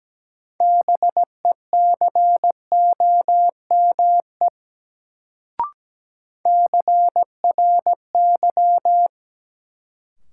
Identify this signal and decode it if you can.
Morse code, heard as a keyed tone, about 17 wpm, 700 Hz. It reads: BECOME CRY
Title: Sets of 5 Words Encore - New Word Drill - 17wpm